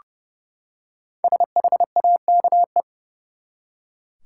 Morse code, heard as a keyed tone, about 30 wpm, 700 Hz. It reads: SHAKE
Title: Five Letter Words - 30wpm